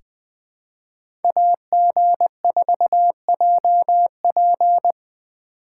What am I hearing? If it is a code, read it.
Morse code, heard as a keyed tone, about 20 wpm, 700 Hz. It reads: AG4JP
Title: USA Call Signs - 5 Characters - 6x Character Spacing - 20wpm